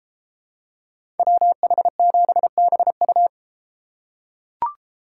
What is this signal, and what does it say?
Morse code, heard as a keyed tone, about 30 wpm, 700 Hz. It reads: WH7BU